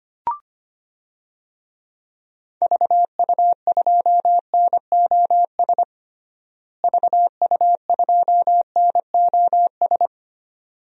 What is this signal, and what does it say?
Morse code, heard as a keyed tone, about 25 wpm, 700 Hz. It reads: VU2NOS VU2NOS
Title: DXCC-100 Call Signs - 6 Characters - Repeated 2x - 25wpm